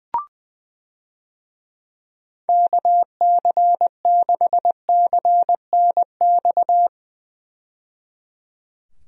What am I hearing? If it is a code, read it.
Morse code, heard as a keyed tone, about 20 wpm, 700 Hz. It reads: KC6CNX
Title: USA Call Signs - 6 Characters - 20wpm